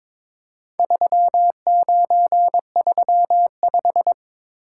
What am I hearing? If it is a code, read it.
Morse code, heard as a keyed tone, about 22 wpm, 700 Hz. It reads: 3935
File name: practice / 22wpm / Four Letter-Number Combinations - 22wpm